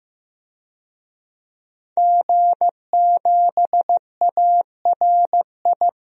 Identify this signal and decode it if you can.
Morse code, heard as a keyed tone, about 15 wpm, 700 Hz. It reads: G7ARI
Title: DXCC-100 Call Signs - 5 Characters - Repeated 3x - 15wpm